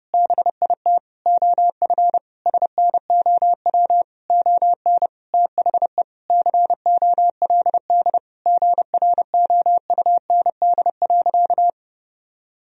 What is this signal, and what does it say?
Morse code, heard as a keyed tone, about 30 wpm, 700 Hz. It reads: BIT OF SNOW ON THE COLD GROUND.
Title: Sentences from Top 1000 Words - Review Part 3 - 30wpm